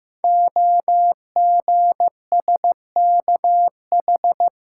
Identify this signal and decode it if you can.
Morse code, heard as a keyed tone, about 15 wpm, 700 Hz. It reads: OGSKH
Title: Code Groups - 5 Letters - 15wpm